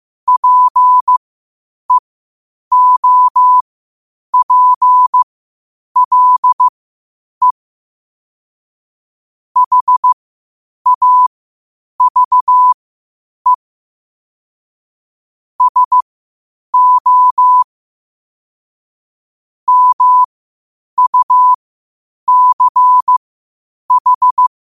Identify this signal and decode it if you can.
Morse code, heard as a keyed tone, about 15 wpm, spaced as for 8 wpm, 1000 Hz. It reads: PEOPLE HAVE SO MUCH
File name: qotdfiles / 8wpm